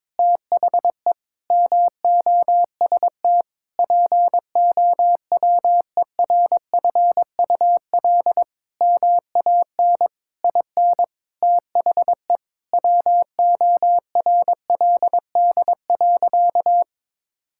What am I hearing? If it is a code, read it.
Morse code, heard as a keyed tone, about 22 wpm, 700 Hz. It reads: THE MOST POWERFUL MAN IN THE WORLD.
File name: Sentences with 7-Word Phrases - 22wpm